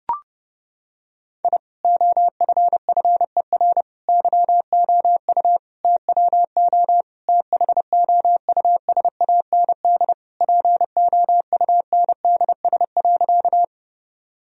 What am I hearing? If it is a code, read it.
Morse code, heard as a keyed tone, about 30 wpm, 700 Hz. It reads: I OFFER YOU TWO THOUSAND POUNDS.
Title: Sentences from Top 1000 Words - Review Part 4 - 30wpm